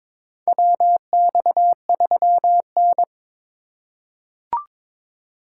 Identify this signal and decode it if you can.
Morse code, heard as a keyed tone, about 22 wpm, 700 Hz. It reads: WX3N
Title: USA Call Signs - 4 Characters - 2x Character Spacing - 22wpm